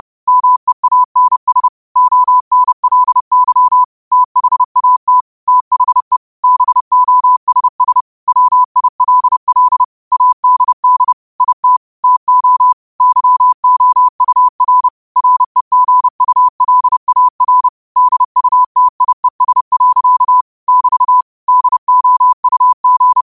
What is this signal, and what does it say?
Morse code, heard as a keyed tone, about 30 wpm, 1000 Hz. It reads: MEANS ONLY THAT THE BOSS WILL ADD IT TO YOUR REGULAR DUTIES. <BT> DOUG
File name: qotdfiles / 30wpm